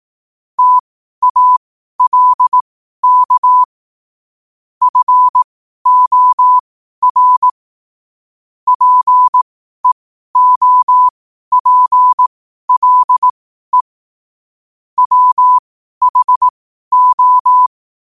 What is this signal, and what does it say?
Morse code, heard as a keyed tone, about 18 wpm, spaced as for 12 wpm, 1000 Hz. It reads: TALK FOR PEOPLE WHO
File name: qotdfiles / 12wpm